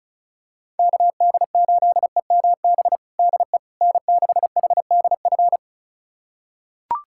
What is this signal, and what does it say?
Morse code, heard as a keyed tone, about 36 wpm, 700 Hz. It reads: KD8EMB DE N6HDF
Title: USA Callsign DE Callsign 35wpm